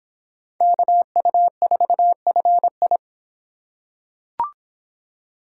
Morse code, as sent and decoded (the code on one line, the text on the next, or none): -.- ..- ....- ..-. ..
KU4FI